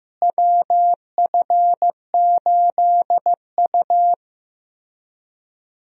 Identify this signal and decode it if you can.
Morse code, heard as a keyed tone, about 15 wpm, 700 Hz. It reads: WF8U
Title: USA Call Signs - 4 Characters - Repeated 2x - 15wpm